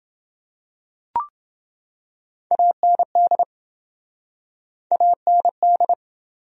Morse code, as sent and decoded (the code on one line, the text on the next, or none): .- -. -.. / .- -. -..
AND AND